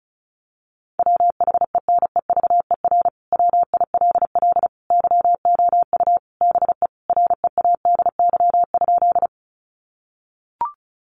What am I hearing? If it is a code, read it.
Morse code, heard as a keyed tone, about 34 wpm, 700 Hz. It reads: WHENEVER WILL YOU BE READY?
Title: Sentences from Top 1200 Words 35wpm